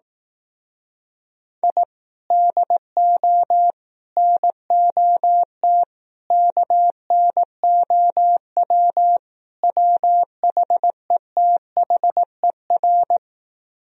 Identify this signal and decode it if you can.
Morse code, heard as a keyed tone, about 18 wpm, 700 Hz. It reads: I DO NOT KNOW WHETHER
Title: Speed-Racing - Sets of 5 Words 15wpm